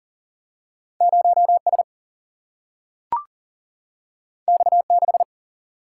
Morse code, heard as a keyed tone, about 41 wpm, 700 Hz. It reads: 0S X6
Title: Single-Letter Single-Number 40wpm